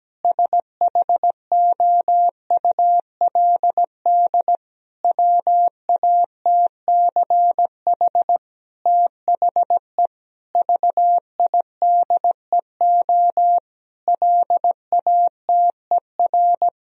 Morse code, heard as a keyed tone, about 17 wpm, 700 Hz. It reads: SHOULD WATCH THE VIDEO LATER
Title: Sentences from Top 1000 Words - Review Part 4 - 17wpm